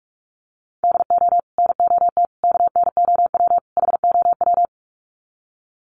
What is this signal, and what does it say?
Morse code, heard as a keyed tone, about 44 wpm, 700 Hz. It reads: DO NOT KNOW HOW
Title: Sets of 4 Words 45wpm